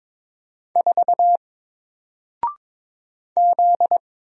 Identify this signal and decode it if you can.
Morse code, heard as a keyed tone, about 22 wpm, 700 Hz. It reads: 4 Z
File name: Single-Letter-Number-22wpm